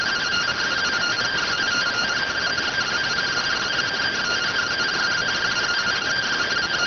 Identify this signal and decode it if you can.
Two-tone teleprinter signal, about 170 Hz shift.